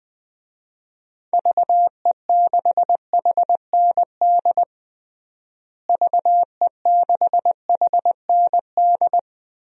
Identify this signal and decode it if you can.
Morse code, heard as a keyed tone, about 21 wpm, 700 Hz. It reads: VE6HND VE6HND
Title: DXCC-100 Call Signs - 6 Characters - Repeated 2x - 20wpm